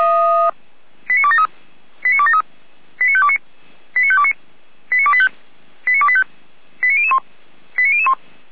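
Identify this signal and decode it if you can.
Five-tone selective call (ZVEI), heard as quick runs of five tones, tones of about 70 ms. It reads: 89273 89273 89529 89529 89285 89285 89001 89001
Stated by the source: ZVEI five-tone selective call received by SDR.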